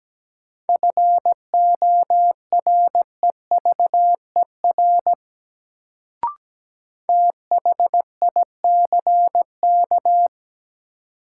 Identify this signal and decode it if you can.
Morse code, heard as a keyed tone, about 17 wpm, 700 Hz. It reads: FOREVER THICK
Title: Sentences from Top 1600 Words - New Word Drill Alt - 17wpm